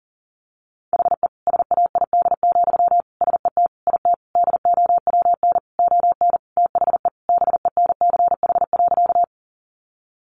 Morse code, heard as a keyed tone, about 40 wpm, 700 Hz. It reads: HE SAID, SET IT DOWN ON THE BENCH.
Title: Sentences from Top 1400 Words 40wpm